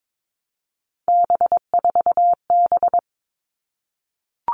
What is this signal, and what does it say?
Morse code, heard as a keyed tone, about 22 wpm, 700 Hz. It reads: B4B
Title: All Cut Numbers - 3-Digit 22wpm